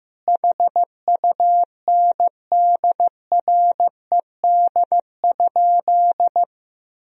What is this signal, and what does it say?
Morse code, heard as a keyed tone, about 16 wpm, 700 Hz. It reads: HUNDRED?
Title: Sentences from Top 400 Words and Abbreviations 15wpm